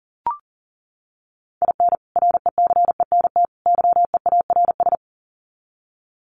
Morse code, heard as a keyed tone, about 40 wpm, 700 Hz. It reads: IN RECENT YEARS
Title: Sets of 3 Words 40wpm